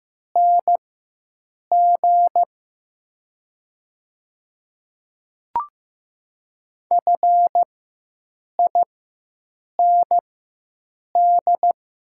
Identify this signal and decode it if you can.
Morse code, heard as a keyed tone, about 15 wpm, spaced as for 7 wpm, 700 Hz. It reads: NG FIND
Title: 4x Character Spacing - Top 100 Words - 15wpm